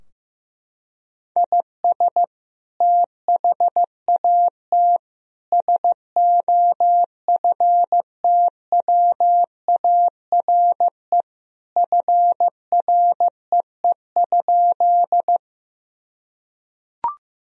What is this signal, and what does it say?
Morse code, heard as a keyed tone, about 15 wpm, 700 Hz. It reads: IS THAT SOFTWARE FREE?
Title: Sentences from Top 1000 Words - Review Part 3 - 15wpm